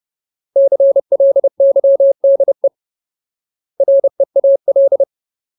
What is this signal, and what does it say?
Morse code, heard as a keyed tone, about 30 wpm, 550 Hz. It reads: CLYDE REAL
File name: Ultimate Word List / 30WPM Ultimate